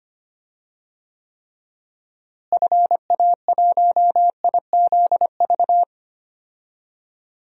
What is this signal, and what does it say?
Morse code, heard as a keyed tone, about 25 wpm, 700 Hz. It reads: FA1IZV
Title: DXCC-100 Call Signs - 6 Characters - Repeated 3x - 25wpm